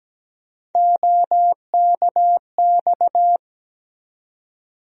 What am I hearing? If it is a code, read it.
Morse code, heard as a keyed tone, about 17 wpm, 700 Hz. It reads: OKX
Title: USA Call Sign Postfixes - 3 Letters 17wpm